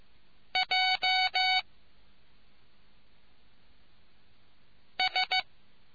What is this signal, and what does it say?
Morse code, heard as a keyed tone, about 15 wpm, 750 Hz. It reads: J S